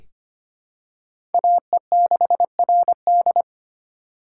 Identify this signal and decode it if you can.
Morse code, heard as a keyed tone, about 25 wpm, 700 Hz. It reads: AE6RD
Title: USA Call Signs - 5 Characters - 3x Character Spacing - 25wpm